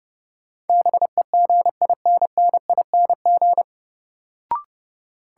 Morse code, heard as a keyed tone, about 32 wpm, 700 Hz. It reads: BEGINNING
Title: Sentences from Top 800 Words - New Word Drill - 30wpm